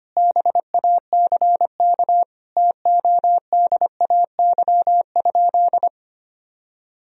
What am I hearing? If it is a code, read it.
Morse code, heard as a keyed tone, about 25 wpm, 700 Hz. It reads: BACK TODAY?
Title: Sentences from Top 1000 Words - Review Part 4 - 25wpm